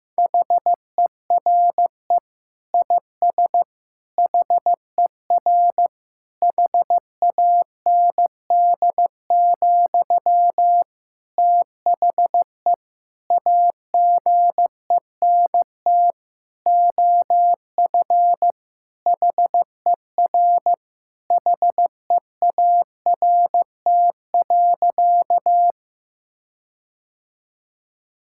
Morse code, heard as a keyed tone, about 15 wpm, 700 Hz. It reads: HERE IS HER HAND, THE AGENT OF HER HEART.
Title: Sentences from Top 1600 Words 15wpm